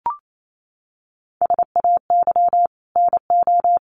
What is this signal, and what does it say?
Morse code, heard as a keyed tone, about 28 wpm, 700 Hz. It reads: SAY NO
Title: Sentences from Top 800 Words 28wpm